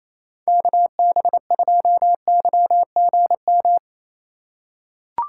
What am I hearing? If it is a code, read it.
Morse code, heard as a keyed tone, about 27 wpm, 700 Hz. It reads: KB2YGM
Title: USA Call Signs - 6 Characters - 4x Character Spacing - 28wpm